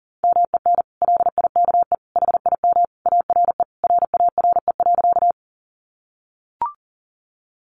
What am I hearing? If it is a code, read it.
Morse code, heard as a keyed tone, about 40 wpm, 700 Hz. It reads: MEN LIKE HIM ARE RARE.
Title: Sentences from Top 1100 Words 40wpm